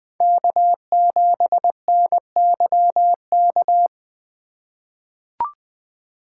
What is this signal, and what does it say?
Morse code, heard as a keyed tone, about 20 wpm, 700 Hz. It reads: K7NYK
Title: USA Call Signs - 2x Character Spacing - 20wpm